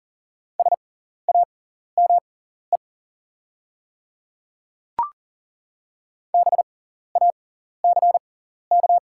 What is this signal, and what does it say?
Morse code, heard as a keyed tone, about 40 wpm, spaced as for 14 wpm, 700 Hz. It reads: SAME BACK
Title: 6x Character Spacing - Top 300 Words - 40wpm